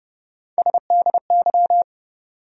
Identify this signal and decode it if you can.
Morse code, heard as a keyed tone, about 31 wpm, 700 Hz. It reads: SDY
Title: USA Call Sign Postfixes - 3 Letters 30wpm